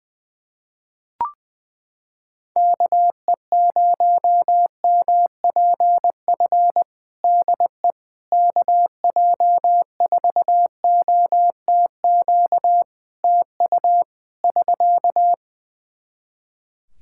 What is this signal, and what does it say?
Morse code, heard as a keyed tone, about 20 wpm, 700 Hz. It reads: KE0MPF DE KJ4OTQ TU <SK>
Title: Practice QSO - Exchange 4 - Ending - 20wpm